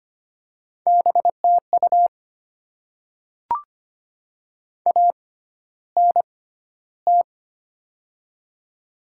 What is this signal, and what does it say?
Morse code, heard as a keyed tone, about 25 wpm, spaced as for 13 wpm, 700 Hz. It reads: BTU ANT
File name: Top 20 QSO Elements - 6x Character Spacing - 25wpm